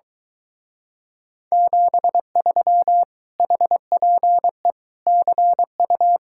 Morse code, heard as a keyed tone, about 23 wpm, 700 Hz. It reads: 73 HPE CU